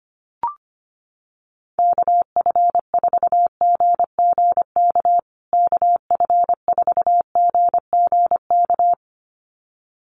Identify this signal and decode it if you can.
Morse code, heard as a keyed tone, about 25 wpm, 700 Hz. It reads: KF4GGK KF4GGK
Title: USA Call Signs Repeated 2x 25wpm